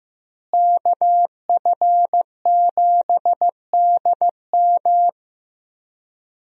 Morse code, heard as a keyed tone, about 15 wpm, 700 Hz. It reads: KF7DM
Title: USA Call Signs - 5 Characters - Repeated 3x - 15wpm